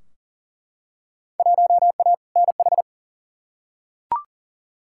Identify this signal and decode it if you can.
Morse code, heard as a keyed tone, about 37 wpm, 700 Hz. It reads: 1A NH
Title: Field Day Exchanges Encore - 40wpm